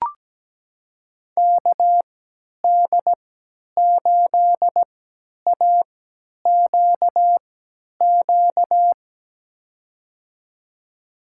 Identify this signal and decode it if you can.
Morse code, heard as a keyed tone, about 17 wpm, spaced as for 10 wpm, 700 Hz. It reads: KD8AQQ
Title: USA Call Signs - 6 Characters - 3x Character Spacing - 17wpm